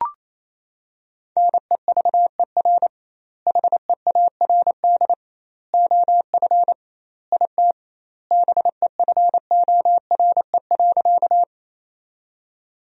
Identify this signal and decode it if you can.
Morse code, heard as a keyed tone, about 28 wpm, 700 Hz. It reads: NEVER HEARD OF IT BEFORE.
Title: 2x Word-Spacing - Sentences from Top 400 Words - 28wpm